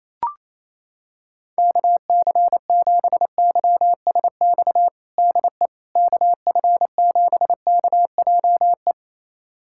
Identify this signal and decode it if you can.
Morse code, heard as a keyed tone, about 28 wpm, 700 Hz. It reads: KC7YSX DE KF7KJE
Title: USA Callsign DE Callsign - 6 Characters - 28wpm